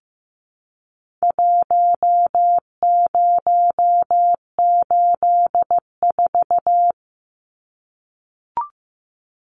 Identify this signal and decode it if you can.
Morse code, heard as a keyed tone, about 15 wpm, 700 Hz. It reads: 1084
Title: Numbers - Contest Style - 4-Digit 15wpm